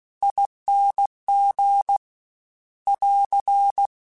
Morse code, heard as a keyed tone, about 16 wpm, 800 Hz. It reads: ING <AR>
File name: Difficult Combos / Difficult Combos-mp3s / T AOEI @15